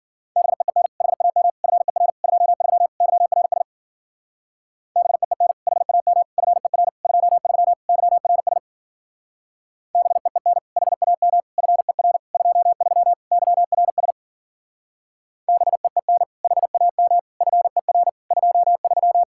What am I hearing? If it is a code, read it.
Morse code, heard as a keyed tone, about 52 wpm, 700 Hz. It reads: BEEN HAM FER 23 YRS BEEN HAM FER 23 YRS BEEN HAM FER 23 YRS BEEN HAM FER 23